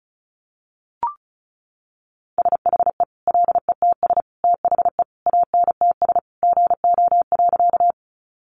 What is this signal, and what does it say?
Morse code, heard as a keyed tone, about 34 wpm, 700 Hz. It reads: SHE LETS THE ANTS GO.
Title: Easy Phrases 35wpm